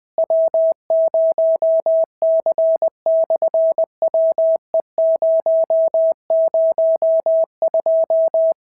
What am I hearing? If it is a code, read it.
Morse code, heard as a keyed tone, about 20 wpm, 660 Hz. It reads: W0C/WE002
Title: SOTA QSO Practice 20wpm